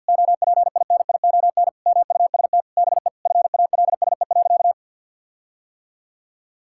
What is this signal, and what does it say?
Morse code, heard as a keyed tone, about 51 wpm, 700 Hz. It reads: OPINION MUST BE FALSE.
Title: Sentences from Top 1600 Words 50wpm